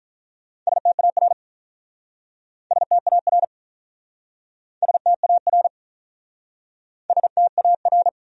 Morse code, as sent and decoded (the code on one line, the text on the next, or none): ... - .- .-. / ... - .- .-. / ... - .- .-. / ... - .- .-.
STAR STAR STAR STAR